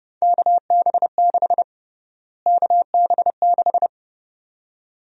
Morse code, heard as a keyed tone, about 30 wpm, 700 Hz. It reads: KB6 KB6
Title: USA Call Sign Prefixes - Repeated 2x - 30wpm